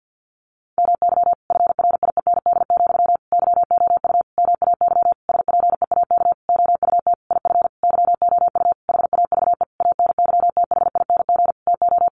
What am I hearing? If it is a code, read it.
Morse code, heard as a keyed tone, about 50 wpm, 700 Hz. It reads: MY FRIEND, YOU MAY SPEAK OUT IF YOU HAVE ANYTHING TO
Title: Sentences from Top 1300 Words 50wpm